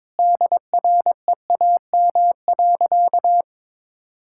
Morse code, heard as a keyed tone, about 22 wpm, 700 Hz. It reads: DREAM.
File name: practice / 22wpm / Sentences with 4-Word Phrases - 22wpm